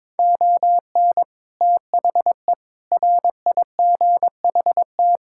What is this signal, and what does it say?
Morse code, heard as a keyed tone, about 22 wpm, 700 Hz. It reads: ON THE RIGHT
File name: practice / 22wpm / Sets of 6 Words - 22wpm